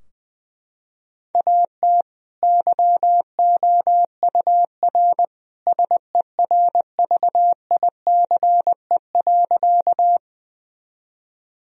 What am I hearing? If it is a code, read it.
Morse code, heard as a keyed tone, about 21 wpm, 700 Hz. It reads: AT YOUR SERVICE.
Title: Sentences from Top 1000 Words - Review Part 4 - 20wpm